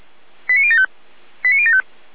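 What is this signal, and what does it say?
Five-tone selective call (ZVEI), heard as quick runs of five tones, tones of about 70 ms. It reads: 89075 89075
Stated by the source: ZVEI five-tone selective call received by SDR, from the feuerwehr-orlamuende.org fire brigade archive.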